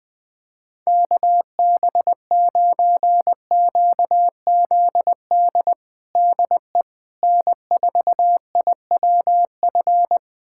Morse code, heard as a keyed tone, about 20 wpm, 700 Hz. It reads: KB9QZD DE N4IWF
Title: USA Callsign DE Callsign 20wpm